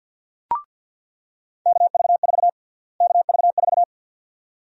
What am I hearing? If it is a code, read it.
Morse code, heard as a keyed tone, about 46 wpm, 700 Hz. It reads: KV4 KV4
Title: USA Call Sign Prefixes - Repeated 2x - 50wpm